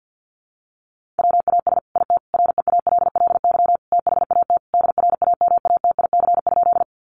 Morse code, heard as a keyed tone, about 50 wpm, 700 Hz. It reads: WAS IT REALLY THAT DRAMATIC?
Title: Sentences from Top 1300 Words 50wpm